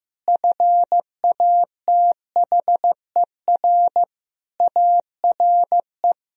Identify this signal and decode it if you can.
Morse code, heard as a keyed tone, about 15 wpm, 700 Hz. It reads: FATHER ARE
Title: Sentences from Top 1000 Words - Review Part 4 - 15wpm